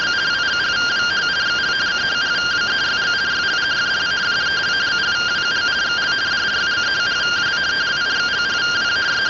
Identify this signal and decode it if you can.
Two-tone teleprinter signal, about 170 Hz shift.